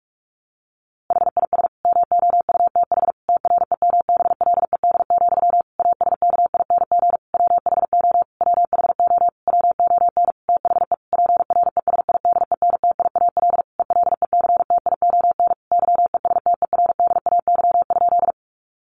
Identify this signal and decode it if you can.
Morse code, heard as a keyed tone, about 44 wpm, 700 Hz. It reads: HIS MOUTH TREMBLED, ASKING WHO WHO WON THE PRESIDENTIAL ELECTION YESTERDAY?
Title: A Sound Of Thunder 45wpm